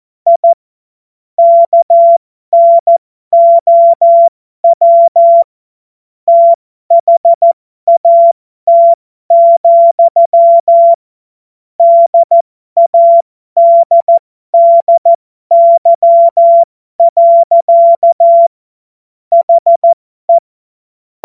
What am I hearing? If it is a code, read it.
Morse code, heard as a keyed tone, about 14 wpm, spaced as for 12 wpm, 680 Hz. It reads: I KNOW THAT, DADDY. HE
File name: Morsekurs / Witze